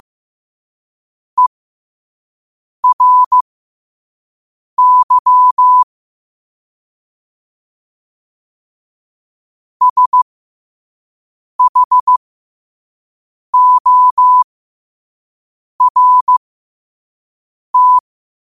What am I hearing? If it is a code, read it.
Morse code, heard as a keyed tone, about 15 wpm, spaced as for 5 wpm, 1000 Hz. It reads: ERY SHORT
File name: qotdfiles / 5wpm